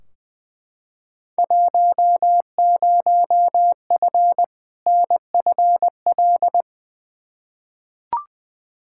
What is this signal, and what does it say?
Morse code, heard as a keyed tone, about 20 wpm, 700 Hz. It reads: 10F NFL
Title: Field Day Exchanges Encore - 20wpm